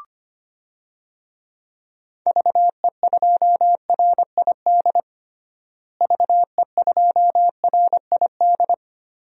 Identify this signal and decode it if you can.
Morse code, heard as a keyed tone, about 25 wpm, 700 Hz. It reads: VE2RID VE2RID
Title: DXCC-100 Call Signs - 6 Characters - Repeated 3x - 25wpm